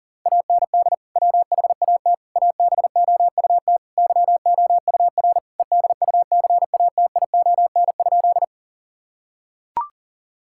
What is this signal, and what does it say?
Morse code, heard as a keyed tone, about 40 wpm, 700 Hz. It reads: AND WHAT ABOUT YOUR EDUCATION?
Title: Sentences from Top 1000 Words - Review Part 6 - 40wpm